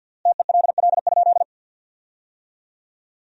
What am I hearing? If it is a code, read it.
Morse code, heard as a keyed tone, about 50 wpm, 700 Hz. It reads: TELL?
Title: 3x Word-Spacing - Sentences from Top 300 Words - 50wpm